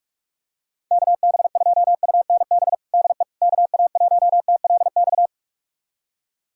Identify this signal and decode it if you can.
Morse code, heard as a keyed tone, about 46 wpm, 700 Hz. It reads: KB2UNB DE KA1TLX
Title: USA Callsign DE Callsign - 6 Characters - 45wpm